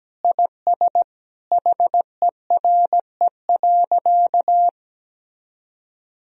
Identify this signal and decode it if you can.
Morse code, heard as a keyed tone, about 17 wpm, 700 Hz. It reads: IS HERE.
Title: Sentences from Top 1000 Words - Review Part 3 - 17wpm